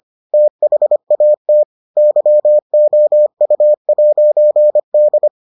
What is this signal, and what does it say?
Morse code, heard as a keyed tone, about 25 wpm, 600 Hz. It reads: THAT YOU'D